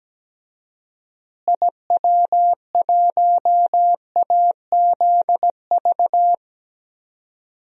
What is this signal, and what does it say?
Morse code, heard as a keyed tone, about 17 wpm, 700 Hz. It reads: IW1AZV
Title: DXCC-100 Call Signs - 6 Characters - Repeated 3x - 17wpm